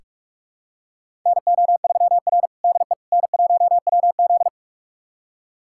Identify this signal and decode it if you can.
Morse code, heard as a keyed tone, about 42 wpm, 700 Hz. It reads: NO3R DE N1WZ
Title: USA Callsign DE Callsign - 4 Characters - 45wpm